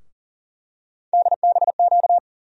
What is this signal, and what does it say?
Morse code, heard as a keyed tone, about 40 wpm, 700 Hz. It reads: DBQ
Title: USA Call Sign Postfixes - 3 Letters 40wpm